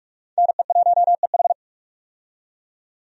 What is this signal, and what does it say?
Morse code, heard as a keyed tone, about 41 wpm, 700 Hz. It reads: NE1EH
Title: USA Call Signs - 5 Characters - 45wpm